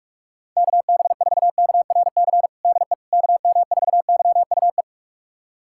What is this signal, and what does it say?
Morse code, heard as a keyed tone, about 42 wpm, 700 Hz. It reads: KB4KRC DE KM4YUE
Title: USA Callsign DE Callsign - 6 Characters - 45wpm